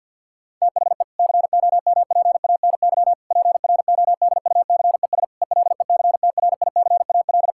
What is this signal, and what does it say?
Morse code, heard as a keyed tone, about 50 wpm, 700 Hz. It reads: THE COMPANY PRODUCES ELECTRICAL
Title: Sentences from Top 1000 Words - Review Part 6 - 50wpm